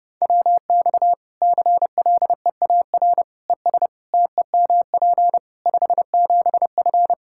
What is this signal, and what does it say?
Morse code, heard as a keyed tone, about 30 wpm, 700 Hz. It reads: WX CLEAR ES TEMP 57F
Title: Practice QSO - Exchange 2 - Info - 30wpm